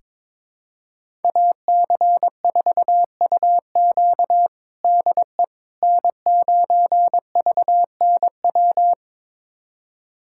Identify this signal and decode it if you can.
Morse code, heard as a keyed tone, about 22 wpm, 700 Hz. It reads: AC4UQ DE N9VNW
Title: USA Callsign DE Callsign - 5 Characters - 22wpm